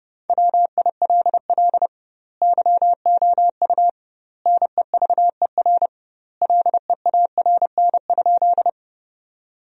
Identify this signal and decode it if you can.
Morse code, heard as a keyed tone, about 32 wpm, 700 Hz. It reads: WILL YOU NEVER LEARN?
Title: 2x Word-Spacing - Sentences from Top 300 Words - 30wpm